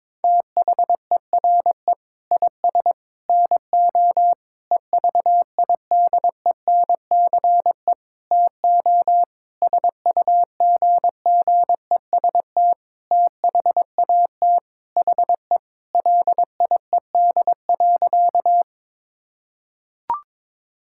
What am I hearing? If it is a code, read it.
Morse code, heard as a keyed tone, about 22 wpm, 700 Hz. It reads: THERE IS NO EVIDENCE TO SUGGEST THAT HE LIED.
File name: Sentences with 7-Word Phrases - 22wpm